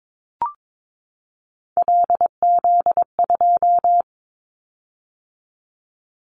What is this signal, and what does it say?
Morse code, heard as a keyed tone, about 22 wpm, 700 Hz. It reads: LZ2
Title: Callsign Prefixes for Top 50 DXCC - 22wpm